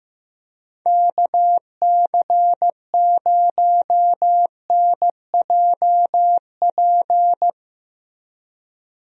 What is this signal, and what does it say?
Morse code, heard as a keyed tone, about 15 wpm, 700 Hz. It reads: KC0NJP
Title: USA Call Signs - 6 Characters - Repeated 2x - 15wpm